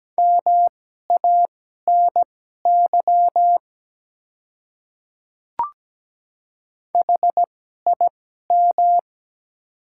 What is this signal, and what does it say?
Morse code, heard as a keyed tone, about 17 wpm, spaced as for 12 wpm, 700 Hz. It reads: MANY HIM
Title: 2x Character Spacing - Top 300 Words - 17wpm